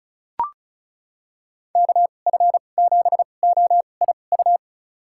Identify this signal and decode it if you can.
Morse code, heard as a keyed tone, about 36 wpm, spaced as for 25 wpm, 700 Hz. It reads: KF7OIU
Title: USA Call Signs - 2x Character Spacing - 35wpm